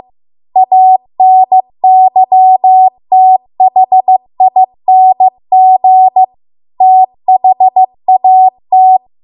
Morse code, heard as a keyed tone, about 15 wpm, 750 Hz. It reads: ANYTHING THAT